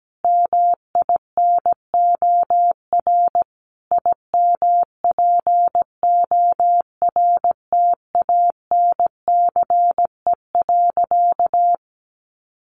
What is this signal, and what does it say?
Morse code, heard as a keyed tone, about 17 wpm, 700 Hz. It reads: MINOR IMPORTANCE.
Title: Sentences from Top 1500 Words 17wpm